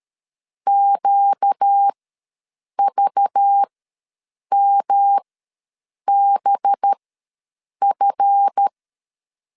Morse code, heard as a keyed tone, about 13 wpm, spaced as for 7 wpm, 800 Hz. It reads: QVMBF
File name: LEZIONI / 10 SIGNS / 10A SLASH ws 12wpm 3443